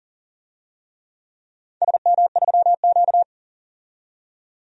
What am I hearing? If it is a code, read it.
Morse code, heard as a keyed tone, about 40 wpm, 700 Hz. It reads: SM3Q
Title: DXCC-100 Call Signs - 4 Characters - 40wpm